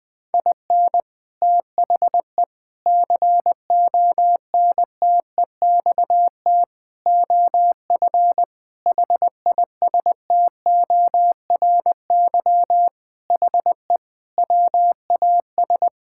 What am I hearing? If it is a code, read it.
Morse code, heard as a keyed tone, about 21 wpm, 700 Hz. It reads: IN THE CONTEXT OF HISTORY HE WAS
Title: Sentences with 4-Word Phrases - Spoken Beginning - 20wpm